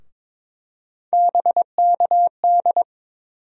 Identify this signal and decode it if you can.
Morse code, heard as a keyed tone, about 22 wpm, 700 Hz. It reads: BKD
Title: USA Call Sign Postfixes - 3 Letters 22wpm